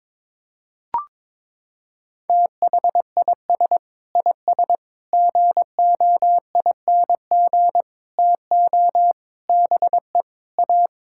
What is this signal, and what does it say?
Morse code, heard as a keyed tone, about 22 wpm, 700 Hz. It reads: THIS IS GOING TO BE A
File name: practice / 22wpm / Sets of 6 Words - 22wpm